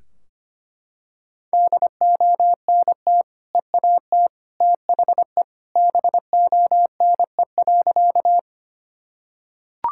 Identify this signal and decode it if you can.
Morse code, heard as a keyed tone, about 25 wpm, 700 Hz. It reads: DONT EAT THE BONE.
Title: Sentences from Top 1600 Words 25wpm